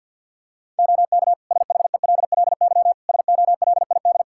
Morse code, heard as a keyed tone, about 53 wpm, 700 Hz. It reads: OK SHELLY SOLID
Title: Practice QSO - Exchange 3 - More Info - 50wpm